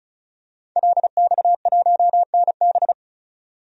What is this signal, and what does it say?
Morse code, heard as a keyed tone, about 34 wpm, 700 Hz. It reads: LX1NB
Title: DXCC-100 Call Signs - 5 Characters - 35wpm